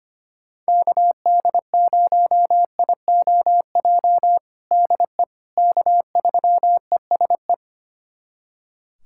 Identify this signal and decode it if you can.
Morse code, heard as a keyed tone, about 25 wpm, 700 Hz. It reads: KD0IOJ DE K3ESE
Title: USA Callsign DE Callsign 25wpm